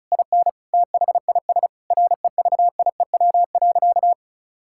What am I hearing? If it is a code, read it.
Morse code, heard as a keyed tone, about 36 wpm, 700 Hz. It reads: IN THIS REVIEW.
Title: Sentences from Top 1000 Words - Review Part 6 - 35wpm